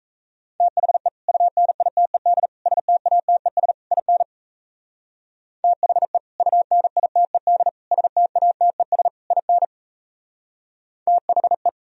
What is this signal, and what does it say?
Morse code, heard as a keyed tone, about 33 wpm, 700 Hz. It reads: THE UNITED STATES IN THE UNITED STATES IN THE
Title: Speed-Racing - Sets of 4 Words 28wpm